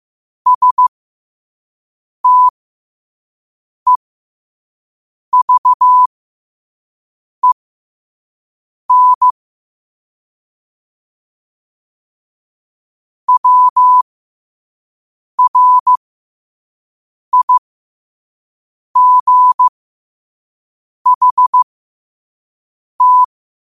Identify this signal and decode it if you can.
Morse code, heard as a keyed tone, about 15 wpm, spaced as for 5 wpm, 1000 Hz. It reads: STEVEN WRIGHT
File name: qotdfiles / 5wpm